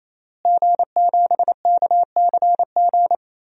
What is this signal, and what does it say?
Morse code, heard as a keyed tone, about 28 wpm, 700 Hz. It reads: G7KCG
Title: DXCC-100 Call Signs - 5 Characters - Repeated 3x - 28wpm